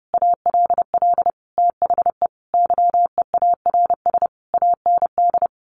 Morse code, heard as a keyed tone, about 30 wpm, 700 Hz. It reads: ALL THE YEARS AND
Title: A Sound Of Thunder 30wpm